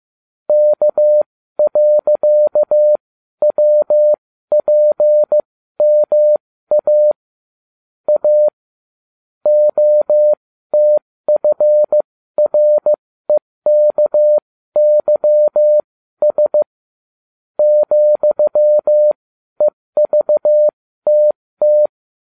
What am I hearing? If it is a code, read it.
Morse code, heard as a keyed tone, about 15 wpm, spaced as for 12 wpm, 600 Hz. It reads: K.WPMA A OTFREKYS ,EVTT
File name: cw / ubacw20 3